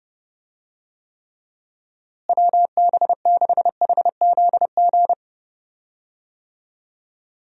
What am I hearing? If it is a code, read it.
Morse code, heard as a keyed tone, about 29 wpm, 700 Hz. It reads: WB6HZG
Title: USA Call Signs - 6 Characters - Repeated 3x - 30wpm